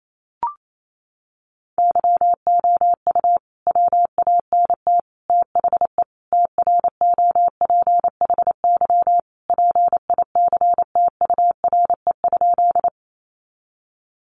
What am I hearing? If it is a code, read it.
Morse code, heard as a keyed tone, about 28 wpm, 700 Hz. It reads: YOU WANT THE TROPHY PICTURE?
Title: A Sound Of Thunder 28wpm